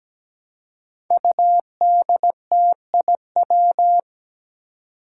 Morse code, heard as a keyed tone, about 17 wpm, 700 Hz. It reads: UDTIW
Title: Five Challenging Letter-Number Combinations 17wpm